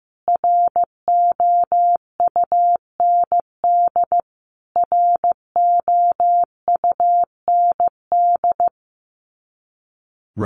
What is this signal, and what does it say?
Morse code, heard as a keyed tone, about 15 wpm, 700 Hz. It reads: ROUND ROUND
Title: Top 500 Words Repeated 15wpm